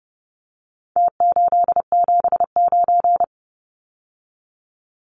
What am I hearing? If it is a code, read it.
Morse code, heard as a keyed tone, about 30 wpm, 700 Hz. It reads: T879
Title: Numbers - Contest Style - 4-Digit 30wpm